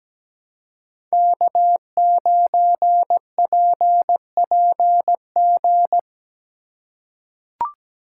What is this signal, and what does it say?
Morse code, heard as a keyed tone, about 17 wpm, 700 Hz. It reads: K9PPG
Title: USA Call Signs - 3x Character Spacing - 17wpm